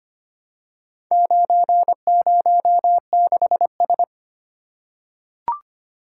Morse code, heard as a keyed tone, about 25 wpm, 700 Hz. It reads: 906S
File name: practice / 25wpm / Four Letter-Number Combinations - 25wpm